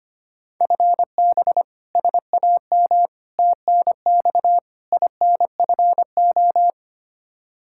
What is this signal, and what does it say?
Morse code, heard as a keyed tone, about 25 wpm, 700 Hz. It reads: FB SAM TNX INFO
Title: Practice QSO - Exchange 3 - More Info - 25wpm